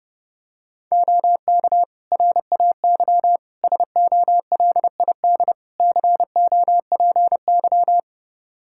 Morse code, heard as a keyed tone, about 30 wpm, 700 Hz. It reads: OK RAY SOLID COPY
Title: Practice QSO - Exchange 3 - More Info - 30wpm